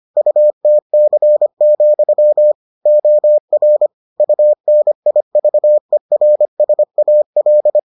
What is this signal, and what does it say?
Morse code, heard as a keyed tone, about 25 wpm, 600 Hz. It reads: UTC, OR UNIVERSAL